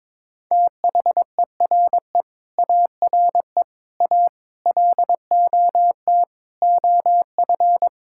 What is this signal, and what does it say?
Morse code, heard as a keyed tone, about 22 wpm, 700 Hz. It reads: THERE ARE A LOT OF
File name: practice / 22wpm / Sets of 7 Words - 22wpm